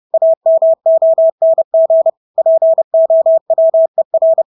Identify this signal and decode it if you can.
Morse code, heard as a keyed tone, about 30 wpm, 650 Hz. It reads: AMONG POWER
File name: Engelske ord 30-30 wpm